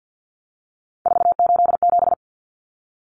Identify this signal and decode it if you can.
Morse code, heard as a keyed tone, about 50 wpm, 700 Hz. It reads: 487
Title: Numbers - 3-Digit 50wpm